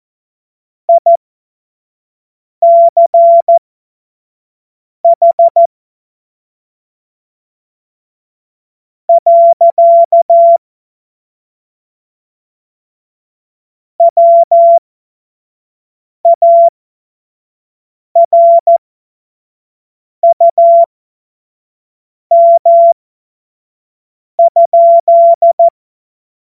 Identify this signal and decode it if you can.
Morse code, heard as a keyed tone, about 14 wpm, spaced as for 4 wpm, 680 Hz. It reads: ICH. WARUM?
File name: Morsekurs / Witze / Witz 11. Mai 2021, 25 F70